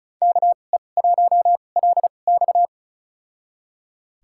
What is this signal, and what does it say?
Morse code, heard as a keyed tone, about 34 wpm, spaced as for 25 wpm, 700 Hz. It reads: KE1LX
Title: USA Call Signs - 2x Character Spacing - 35wpm